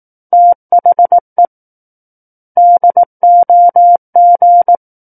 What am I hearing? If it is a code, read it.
Morse code, heard as a keyed tone, about 18 wpm, 700 Hz. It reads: THE DOG
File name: 18 WPM